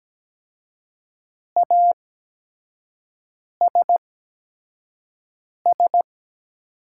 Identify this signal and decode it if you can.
Morse code, heard as a keyed tone, about 17 wpm, spaced as for 5 wpm, 700 Hz. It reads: ASS
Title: 8x Character Spacing - Top 500 Words - 17wpm